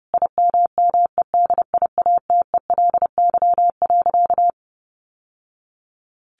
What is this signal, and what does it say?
Morse code, heard as a keyed tone, about 30 wpm, 700 Hz. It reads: IMMEDIATELY.
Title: Sentences from Top 1100 Words 30wpm